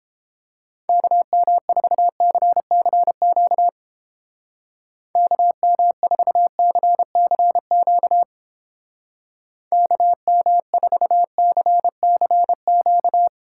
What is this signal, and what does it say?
Morse code, heard as a keyed tone, about 29 wpm, 700 Hz. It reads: KM4CCQ KM4CCQ KM4CCQ